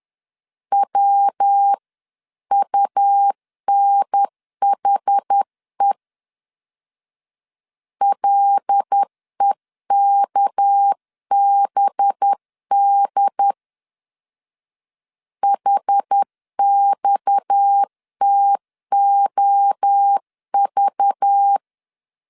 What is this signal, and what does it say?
Morse code, heard as a keyed tone, about 10 wpm, 800 Hz. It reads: WUNHE LEKBD HXTOV
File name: CW / DET / 1 6 1500C S C 10wpm 4252